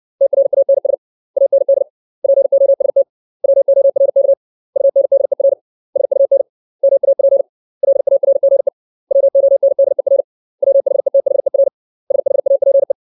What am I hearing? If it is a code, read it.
Morse code, heard as a keyed tone, about 60 wpm, 550 Hz. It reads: TRANS AND POST WORK UNDER SUN MAP LARGE WONDER WHETHER SHAPE
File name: 632 words / 632 Words 60WPM